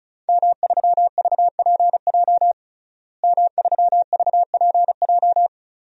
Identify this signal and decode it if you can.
Morse code, heard as a keyed tone, about 36 wpm, 700 Hz. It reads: M3VPJ M3VPJ
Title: DXCC-100 Call Signs - 5 Characters - Repeated 2x - 35wpm